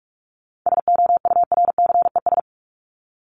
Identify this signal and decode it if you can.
Morse code, heard as a keyed tone, about 44 wpm, 700 Hz. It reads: SOURCES
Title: ES:S Suffix - Top 100 Words with Base Words 45wpm